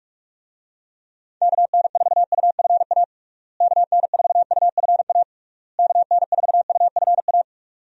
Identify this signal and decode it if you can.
Morse code, heard as a keyed tone, about 46 wpm, 700 Hz. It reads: KN4UFA KN4UFA KN4UFA
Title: USA Call Signs - 6 Characters - Repeated 3x - 45wpm